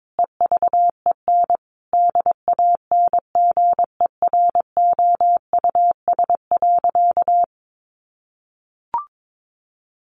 Morse code, heard as a keyed tone, about 22 wpm, 700 Hz. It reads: EVEN DANGEROUS.
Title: Sentences from Top 1200 Words 22wpm